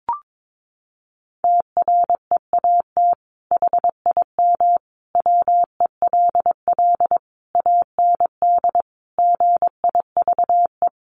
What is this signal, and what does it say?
Morse code, heard as a keyed tone, about 22 wpm, 700 Hz. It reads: TREAT HIM WELL AND GIVE
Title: Sentences from Top 1300 Words 22wpm